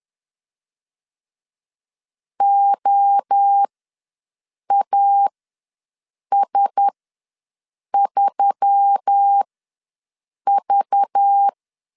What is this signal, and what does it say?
Morse code, heard as a keyed tone, about 10 wpm, spaced as for 6 wpm, 800 Hz. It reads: OAS3V
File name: LEZIONI / 5 LBVC / 5 LBV ws 10wpm 3645